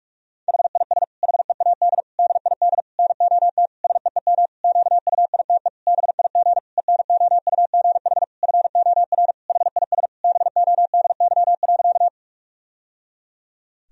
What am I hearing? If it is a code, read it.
Morse code, heard as a keyed tone, about 46 wpm, 700 Hz. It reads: HIS HEAD DID NOT SEEM QUITE BIG ENOUGH FOR HIS BODY.
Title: Sentences from Top 1000 Words - Review Part 2 - 45wpm